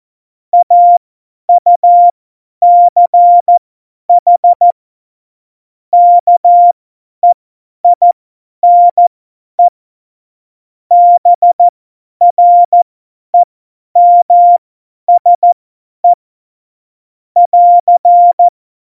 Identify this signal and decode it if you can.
Morse code, heard as a keyed tone, about 14 wpm, spaced as for 10 wpm, 700 Hz. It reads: AUCH KEINE BREMSE <AR>